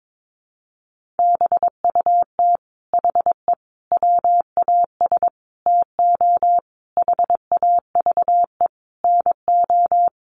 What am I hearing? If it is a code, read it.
Morse code, heard as a keyed tone, about 22 wpm, 700 Hz. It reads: BUT HE WAS TO HAVE NO
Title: Sentences from Top 1500 Words 22wpm